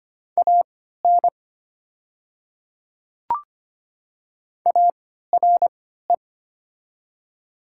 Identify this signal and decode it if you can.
Morse code, heard as a keyed tone, about 25 wpm, spaced as for 14 wpm, 700 Hz. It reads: AN ARE
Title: 3x Character Spacing - Top 100 Words - 25wpm